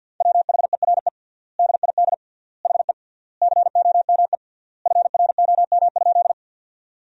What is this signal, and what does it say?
Morse code, heard as a keyed tone, about 50 wpm, 700 Hz. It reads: WHERE DID HE COME FROM?
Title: 3x Word-Spacing - Sentences from Top 200 Words - 50wpm